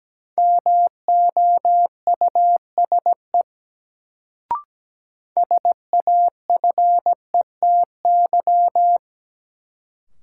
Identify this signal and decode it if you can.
Morse code, heard as a keyed tone, about 17 wpm, 700 Hz. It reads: MOUSE SAFETY
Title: Sentences from Top 700 Words - New Word Drill - 17wpm